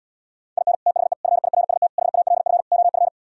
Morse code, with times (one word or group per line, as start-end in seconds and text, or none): 0.6-0.7 IN
0.9-1.1 THE
1.2-1.9 FIRST
2.0-2.6 HALF
2.7-3.1 OF